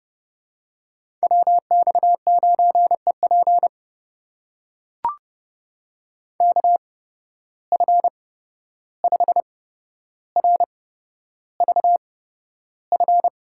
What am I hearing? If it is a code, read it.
Morse code, heard as a keyed tone, about 30 wpm, spaced as for 12 wpm, 700 Hz. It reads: WX9EP KF5RVF
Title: USA Call Signs - 8x Character Spacing - 30wpm